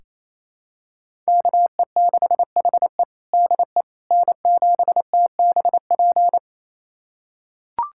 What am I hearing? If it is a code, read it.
Morse code, heard as a keyed tone, about 27 wpm, 700 Hz. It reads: KE6HE DE N7TBP
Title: USA Callsign DE Callsign - 5 Characters - 28wpm